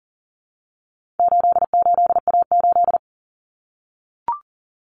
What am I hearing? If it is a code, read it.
Morse code, heard as a keyed tone, about 40 wpm, 700 Hz. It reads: 88A8
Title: Common Cut Numbers - 4-Digit 40wpm